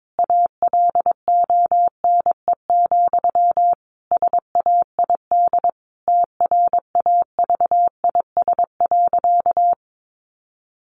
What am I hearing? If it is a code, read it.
Morse code, heard as a keyed tone, about 22 wpm, 700 Hz. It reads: ALONE, SAID TRAVIS.
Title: A Sound Of Thunder 22wpm